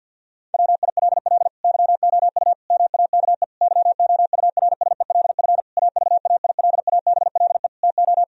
Morse code, heard as a keyed tone, about 51 wpm, 700 Hz. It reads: WILL YOU MAKE YOURSELF AVAILABLE TO
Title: Sentences from Top 1000 Words - Review Part 3 - 50wpm